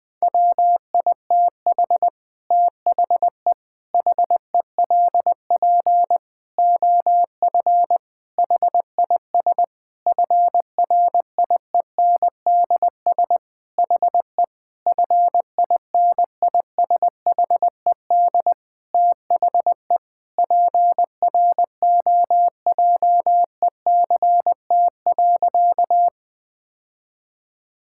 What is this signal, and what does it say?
Morse code, heard as a keyed tone, about 21 wpm, 700 Hz. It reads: WITH THE HELP OF HIS FRIENDS HE FINISHED THE PROJECT.
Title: Sentences with 4-Word Phrases - 20wpm